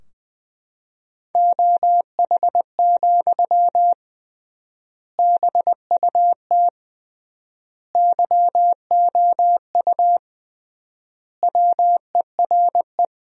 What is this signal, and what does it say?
Morse code, heard as a keyed tone, about 20 wpm, 700 Hz. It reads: OH, BUT YOU WERE
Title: 3x Word-Spacing - Sentences from Top 500 Words - 20wpm